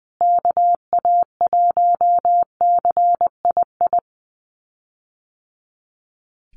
Morse code, heard as a keyed tone, about 20 wpm, 700 Hz. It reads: KA1CII
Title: USA Call Signs Repeated 3x 20wpm